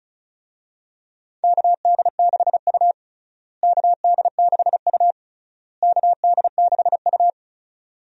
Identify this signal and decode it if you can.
Morse code, heard as a keyed tone, about 36 wpm, 700 Hz. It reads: KD6U KD6U KD6U
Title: USA Call Signs - 4 Characters - Repeated 3x - 35wpm